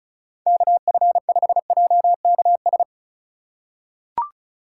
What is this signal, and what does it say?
Morse code, heard as a keyed tone, about 34 wpm, 700 Hz. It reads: KF5JKS
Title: USA Call Signs - 6 Characters - 4x Character Spacing - 35wpm